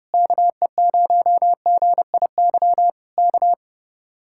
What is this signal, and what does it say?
Morse code, heard as a keyed tone, about 30 wpm, 700 Hz. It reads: KE0GIY K